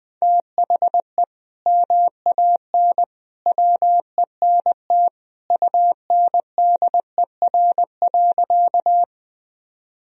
Morse code, heard as a keyed tone, about 20 wpm, 700 Hz. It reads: THE MAN WENT UNDER.
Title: Sentences from Top 1000 Words - Review Part 1 - 20wpm